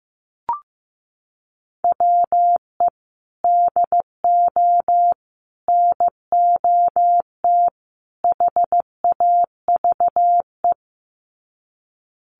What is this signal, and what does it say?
Morse code, heard as a keyed tone, about 15 wpm, 700 Hz. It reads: WE DO NOT HAVE
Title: Sets of 4 Words 15wpm